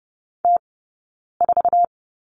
Morse code, heard as a keyed tone, about 30 wpm, spaced as for 9 wpm, 690 Hz. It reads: T4
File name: Lesson 19d - Letter w - Two Characters 30wpm